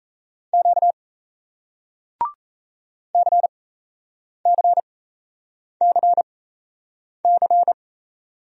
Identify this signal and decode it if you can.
Morse code, heard as a keyed tone, about 38 wpm, spaced as for 8 wpm, 700 Hz. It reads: Q CCCC